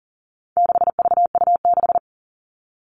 Morse code, heard as a keyed tone, about 39 wpm, 700 Hz. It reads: 6VU6
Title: All Cut Numbers - 4-Digit 40wpm